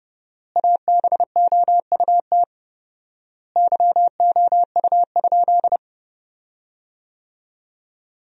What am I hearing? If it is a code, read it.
Morse code, heard as a keyed tone, about 32 wpm, 700 Hz. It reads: ABOUT YOU?
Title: 4x Word-Spacing - Sentences from Top 100 Words - 30wpm